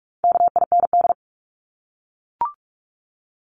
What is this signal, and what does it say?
Morse code, heard as a keyed tone, about 44 wpm, 700 Hz. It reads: KIND
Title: Top 200 Words 45wpm